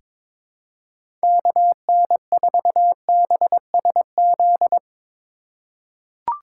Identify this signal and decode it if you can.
Morse code, heard as a keyed tone, about 22 wpm, 700 Hz. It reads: KN4BSZ
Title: USA Call Signs - 6 Characters - 4x Character Spacing - 22wpm